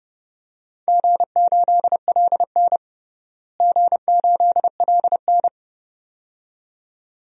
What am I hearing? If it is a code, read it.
Morse code, heard as a keyed tone, about 30 wpm, 700 Hz. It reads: G8LN G8LN
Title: DXCC-100 Call Signs - 4 Characters - Repeated 2x - 30wpm